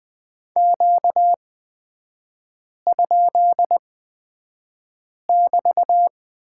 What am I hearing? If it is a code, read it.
Morse code, heard as a keyed tone, about 20 wpm, spaced as for 5 wpm, 700 Hz. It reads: Q?<BT>
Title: 250 Lesson 40e - BK - Three Characters 20wpm